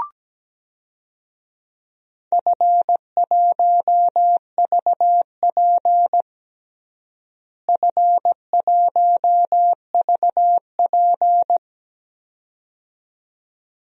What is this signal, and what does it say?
Morse code, heard as a keyed tone, about 17 wpm, 700 Hz. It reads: F1VP F1VP
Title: DXCC-100 Call Signs - 4 Characters - Repeated 2x - 17wpm